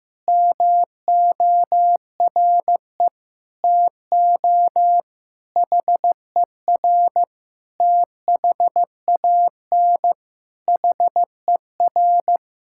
Morse code, heard as a keyed tone, about 15 wpm, 700 Hz. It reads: MORE TO HER THAN HER
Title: Sentences from Top 1000 Words - Review Part 6 - 15wpm